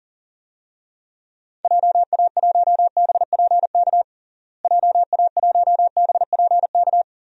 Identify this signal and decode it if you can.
Morse code, heard as a keyed tone, about 40 wpm, 700 Hz. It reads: JA1BPK JA1BPK
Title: DXCC-100 Call Signs - 6 Characters - Repeated 3x - 40wpm